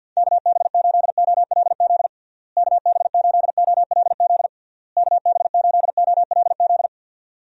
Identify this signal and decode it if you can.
Morse code, heard as a keyed tone, about 52 wpm, 700 Hz. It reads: KB8OLZ KB8OLZ KB8OLZ
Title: USA Call Signs - 6 Characters - Repeated 3x - 50wpm